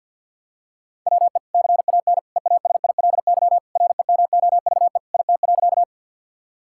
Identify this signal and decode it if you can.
Morse code, heard as a keyed tone, about 47 wpm, 700 Hz. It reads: WE CAN EASILY REMOVE IT.
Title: Sentences from Top 1000 Words - Review Part 5 - 50wpm